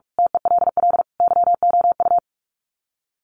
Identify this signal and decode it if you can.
Morse code, heard as a keyed tone, about 44 wpm, 700 Hz. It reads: TELL YOU
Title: Sets of 5 Words 45wpm